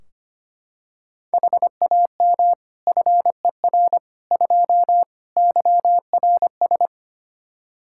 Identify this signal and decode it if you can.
Morse code, heard as a keyed tone, about 25 wpm, 700 Hz. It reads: HAM FER 2 YRS
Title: Practice QSO - Exchange 3 - More Info - 25wpm